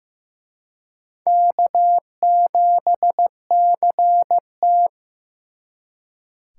Morse code, heard as a keyed tone, about 18 wpm, 700 Hz. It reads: K7CT
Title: Speed Racing - USA 1x2 Call Signs 15wpm